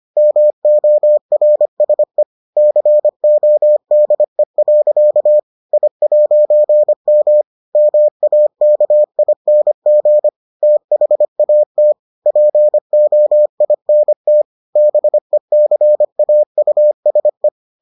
Morse code, heard as a keyed tone, about 25 wpm, 600 Hz. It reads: MORSE CODE. I'M MAKING THAT POINT BECAUSE